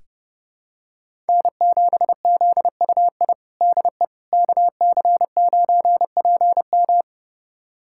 Morse code, heard as a keyed tone, about 32 wpm, 700 Hz. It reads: N7ZUI DE KC9PM
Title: USA Callsign DE Callsign - 5 Characters - 30wpm